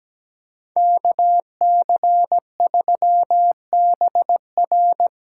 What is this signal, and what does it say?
Morse code, heard as a keyed tone, about 20 wpm, 700 Hz. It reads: KC3BR
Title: Speed Racing - USA Call Signs 17wpm